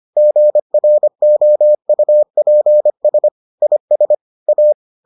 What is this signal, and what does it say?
Morse code, heard as a keyed tone, about 25 wpm, 600 Hz. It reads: GROUPS IS A